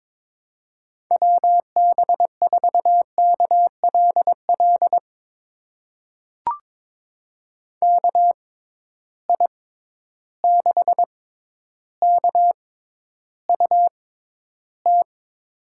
Morse code, heard as a keyed tone, about 22 wpm, spaced as for 11 wpm, 700 Hz. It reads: WB4KLL KI6KUT
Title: USA Call Signs - 6 Characters - 6x Character Spacing - 22wpm